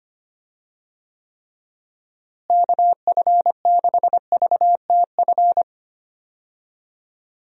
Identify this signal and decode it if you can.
Morse code, heard as a keyed tone, about 25 wpm, 700 Hz. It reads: KF6VTF
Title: USA Call Signs - 6 Characters - Repeated 3x - 25wpm